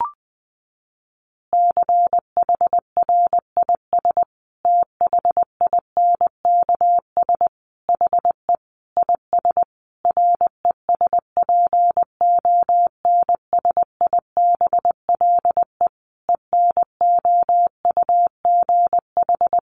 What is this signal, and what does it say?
Morse code, heard as a keyed tone, about 20 wpm, 700 Hz. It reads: CHRIS THINKS HE IS RESPONSIBLE ENOUGH
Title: Easy Phrases 20wpm